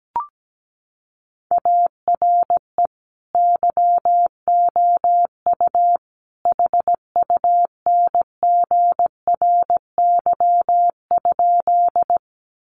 Morse code, heard as a keyed tone, about 17 wpm, 700 Hz. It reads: ARE YOU HUNGRY?
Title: Sentences from Top 1400 Words 17wpm